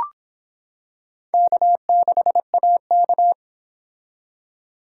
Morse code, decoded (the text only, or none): K6AK